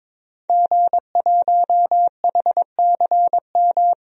Morse code, heard as a keyed tone, about 22 wpm, 700 Hz. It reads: G1HCM
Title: DXCC-100 Call Signs - 5 Characters - 22wpm